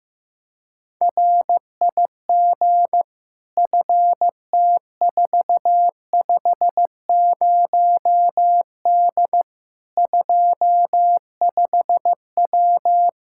Morse code, heard as a keyed tone, about 15 wpm, 700 Hz. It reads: RIG FT450D 25W
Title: Practice QSO - Exchange 2 - Info - 15wpm